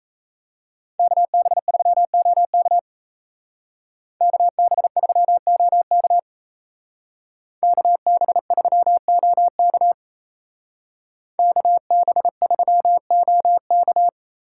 Morse code, heard as a keyed tone, about 33 wpm, 700 Hz. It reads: KB3OK KB3OK KB3OK KB3OK
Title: Speed Racing - USA 2x2 Call Signs 28wpm